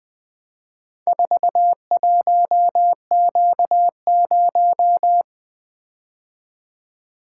Morse code, heard as a keyed tone, about 20 wpm, 700 Hz. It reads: 41Q0
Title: Four Letter-Number Combinations - 20wpm